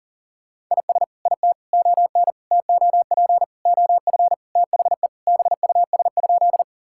Morse code, heard as a keyed tone, about 40 wpm, 700 Hz. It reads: IS IT ON TOP OF THE BUS?
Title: Sentences from Top 1000 Words - Review Part 6 - 40wpm